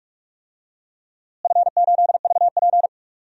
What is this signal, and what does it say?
Morse code, heard as a keyed tone, about 42 wpm, 700 Hz. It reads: F8VP
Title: DXCC-100 Call Signs - 4 Characters - Repeated 3x - 45wpm